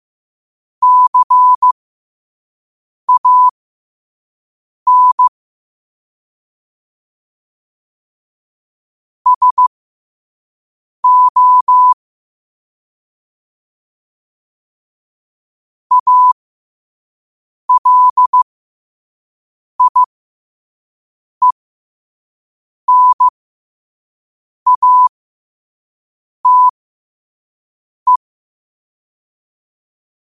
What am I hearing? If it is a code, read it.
Morse code, heard as a keyed tone, about 15 wpm, spaced as for 5 wpm, 1000 Hz. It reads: CAN SO ALIENATE
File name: qotdfiles / 5wpm